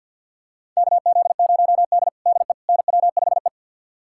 Morse code, heard as a keyed tone, about 51 wpm, 700 Hz. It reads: KZ0D DE NW5E
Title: USA Callsign DE Callsign - 4 Characters - 50wpm